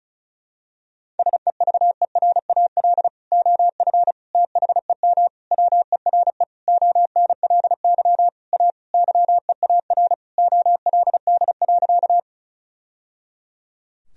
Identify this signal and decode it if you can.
Morse code, heard as a keyed tone, about 34 wpm, 700 Hz. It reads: SEVERAL OF THEM WERE ONLY A YEAR OLD.
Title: Sentences from Top 1000 Words - Review Part 2 - 35wpm